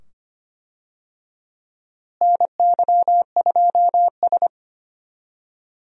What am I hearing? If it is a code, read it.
Morse code, heard as a keyed tone, about 25 wpm, 700 Hz. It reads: NY2S
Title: USA Call Signs - 4 Characters - Repeated 2x - 25wpm